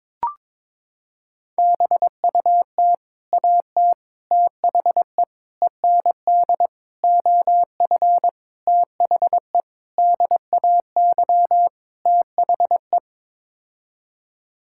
Morse code, heard as a keyed tone, about 22 wpm, 700 Hz. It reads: BUT AT THE END OF THE DAY THE
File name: practice / 22wpm / Sets of 8 Words - 22wpm